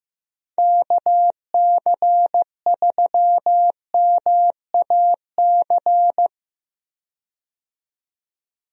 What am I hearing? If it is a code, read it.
Morse code, heard as a keyed tone, about 15 wpm, 700 Hz. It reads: KC3MAC
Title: USA Call Signs - 6 Characters - 15wpm